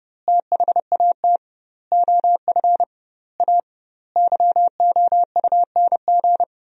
Morse code, heard as a keyed tone, about 30 wpm, 700 Hz. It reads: THAT OF A YOUNG
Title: 2x Word-Spacing - Sentences from Top 500 Words - 30wpm